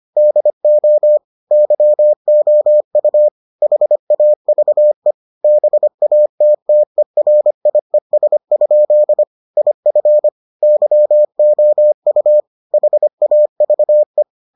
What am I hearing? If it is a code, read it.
Morse code, heard as a keyed tone, about 25 wpm, 600 Hz. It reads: DO YOU HAVE BATTERIES? IF YOU HAVE